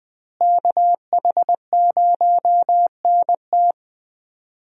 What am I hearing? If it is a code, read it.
Morse code, heard as a keyed tone, about 20 wpm, 700 Hz. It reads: KH0NT
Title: USA Call Signs - 5 Characters - Repeated 2x - 20wpm